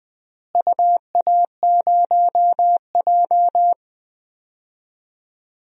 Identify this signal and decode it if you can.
Morse code, heard as a keyed tone, about 20 wpm, 700 Hz. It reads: UA0J
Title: DXCC-100 Call Signs - 4 Characters - Repeated 3x - 20wpm